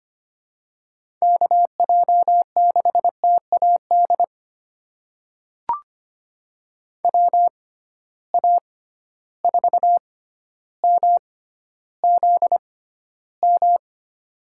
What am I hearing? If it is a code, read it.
Morse code, heard as a keyed tone, about 25 wpm, spaced as for 13 wpm, 700 Hz. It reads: KJ6TAD WA4MZM
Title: USA Call Signs - 6 Characters - 6x Character Spacing - 25wpm